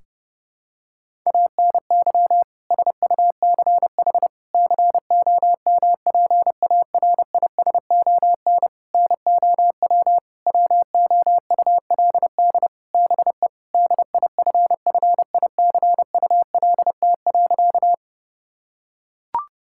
Morse code, heard as a keyed tone, about 32 wpm, 700 Hz. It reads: ANY SUCH COMPARISON NOW WOULD BE DIFFICULT.
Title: Sentences from Top 1600 Words 30wpm